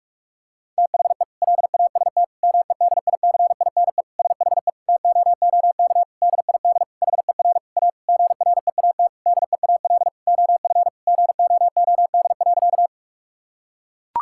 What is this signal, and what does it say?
Morse code, heard as a keyed tone, about 42 wpm, 700 Hz. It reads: THE LAST MEDICINE SHE TOOK DID HER A GREAT DEAL OF GOOD.
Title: Sentences from Top 1000 Words - Review Part 6 - 45wpm